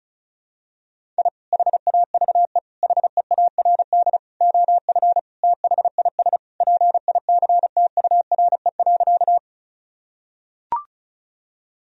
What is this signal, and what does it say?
Morse code, heard as a keyed tone, about 34 wpm, 700 Hz. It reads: I HAVE HEARD OF THIS PICTURE.
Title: Sentences from Top 1000 Words - Review Part 1 - 35wpm